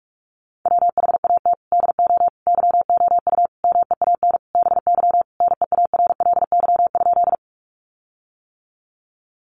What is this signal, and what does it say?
Morse code, heard as a keyed tone, about 44 wpm, 700 Hz. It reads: WHAT DO YOU MEAN BY NEARLY?
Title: Sentences from Top 1300 Words 45wpm